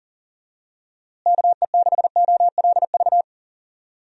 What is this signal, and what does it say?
Morse code, heard as a keyed tone, about 40 wpm, 700 Hz. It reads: KE6OLV
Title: USA Call Signs - 6 Characters - Repeated 3x - 40wpm